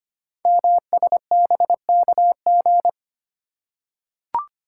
Morse code, heard as a keyed tone, about 25 wpm, 700 Hz. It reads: MSBKG
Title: Code Groups - 5 Letters - 25wpm